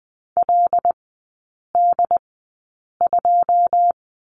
Morse code, heard as a keyed tone, about 20 wpm, spaced as for 8 wpm, 700 Hz. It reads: LD2
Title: 081 Lesson 14e - Number 2 - Three Characters 20wpm